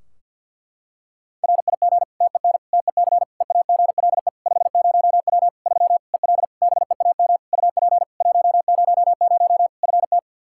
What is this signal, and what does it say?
Morse code, heard as a keyed tone, about 50 wpm, 700 Hz. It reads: RIG TEN TEC EAGLE 50W 3 EL BEAM UP 100 FT
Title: Practice QSO - Exchange 2 - Info - 50wpm